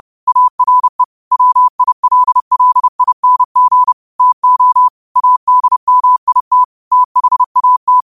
Morse code, heard as a keyed tone, about 30 wpm, 1000 Hz. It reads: ARE WILLING TO ADMIT THAT
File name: qotdfiles / 30wpm